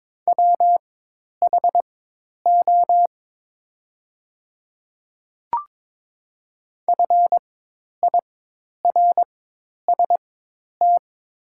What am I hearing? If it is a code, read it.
Morse code, heard as a keyed tone, about 22 wpm, spaced as for 10 wpm, 700 Hz. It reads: WHO FIRST
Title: 4x Character Spacing - Top 300 Words - 22wpm